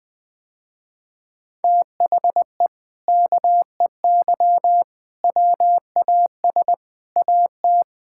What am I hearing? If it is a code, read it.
Morse code, heard as a keyed tone, about 19 wpm, 700 Hz. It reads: THE KEY WAS AT
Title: Sentences with 4-Word Phrases - Spoken Beginning - 20wpm